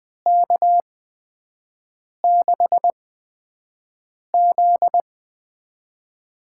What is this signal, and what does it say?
Morse code, heard as a keyed tone, about 20 wpm, spaced as for 5 wpm, 700 Hz. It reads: K6Z
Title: USA Call Signs - 6 Characters - 8x Character Spacing - 20wpm